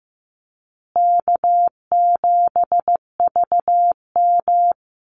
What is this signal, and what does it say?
Morse code, heard as a keyed tone, about 15 wpm, 700 Hz. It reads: K7VM
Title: Call Signs Repeated 2x - 15wpm